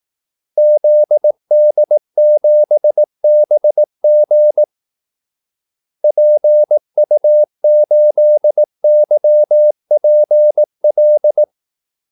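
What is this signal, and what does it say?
Morse code, heard as a keyed tone, about 18 wpm, 600 Hz. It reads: ZD7BG PU8YPL
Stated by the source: Morse call-sign practice at 18 wpm.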